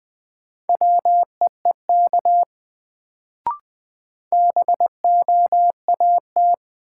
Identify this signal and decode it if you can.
Morse code, heard as a keyed tone, about 20 wpm, 700 Hz. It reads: WEEK BOAT
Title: Sentences from Top 500 Words - New Word Drill - 20wpm